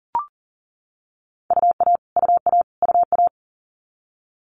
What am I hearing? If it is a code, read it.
Morse code, heard as a keyed tone, about 42 wpm, 690 Hz. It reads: UA UA UA
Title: Multi-letter Phonemes Repeated 3x 40wpm